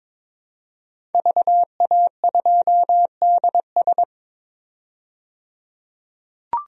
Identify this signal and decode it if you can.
Morse code, heard as a keyed tone, about 22 wpm, 700 Hz. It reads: VA2DS
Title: DXCC-100 Call Signs - 5 Characters - 22wpm